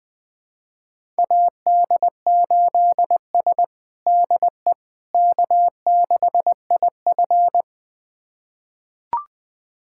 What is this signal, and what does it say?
Morse code, heard as a keyed tone, about 21 wpm, 700 Hz. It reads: AD8S DE K6IF
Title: USA Callsign DE Callsign - 4 Characters - 20wpm